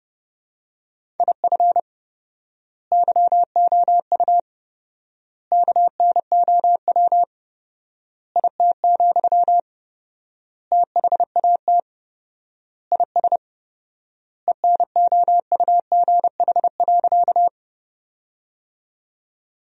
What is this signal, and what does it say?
Morse code, heard as a keyed tone, about 30 wpm, 700 Hz. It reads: IF YOU KNOW IT, THAT IS ENOUGH.
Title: 4x Word-Spacing - Sentences from Top 400 Words - 30wpm